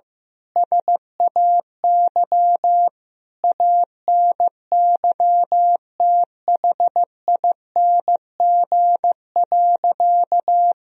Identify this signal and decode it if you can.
Morse code, heard as a keyed tone, about 15 wpm, 700 Hz. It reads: SAY ANYTHING.
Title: Sentences from Top 1000 Words - Review Part 2 - 15wpm